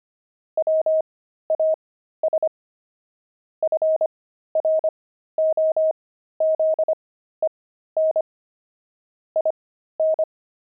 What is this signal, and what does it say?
Morse code, heard as a keyed tone, about 25 wpm, spaced as for 13 wpm, 630 Hz. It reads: WAS FROZEN IN
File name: pangram-fs-13-cs-25-explorer